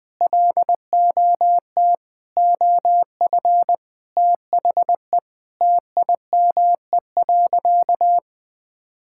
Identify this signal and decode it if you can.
Morse code, heard as a keyed tone, about 20 wpm, 700 Hz. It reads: LOT OF THE TIME.
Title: Sentences with 4-Word Phrases - Spoken Beginning - 20wpm